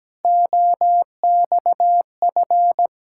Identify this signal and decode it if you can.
Morse code, heard as a keyed tone, about 17 wpm, 700 Hz. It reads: OXF
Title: Ontario QSO Party Exchange 17wpm